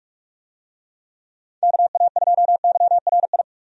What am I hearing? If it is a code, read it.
Morse code, heard as a keyed tone, about 44 wpm, 700 Hz. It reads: KA2YRI
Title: USA Call Signs - 6 Characters - 45wpm